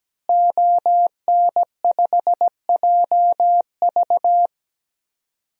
Morse code, heard as a keyed tone, about 17 wpm, 700 Hz. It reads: ON5JV
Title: DXCC-100 Call Signs - 5 Characters - Repeated 2x - 17wpm